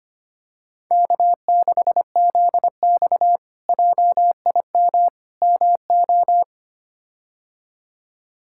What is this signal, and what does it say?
Morse code, heard as a keyed tone, about 25 wpm, 700 Hz. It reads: K6ZX JIM MO
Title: K1USN SST QSO Practice - 25wpm